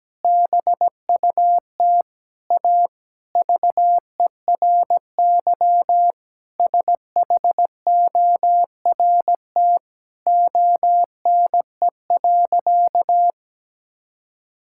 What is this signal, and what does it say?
Morse code, heard as a keyed tone, about 17 wpm, 700 Hz. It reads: BUT A VERY SHORT ONE.
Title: Sentences from Top 1600 Words 17wpm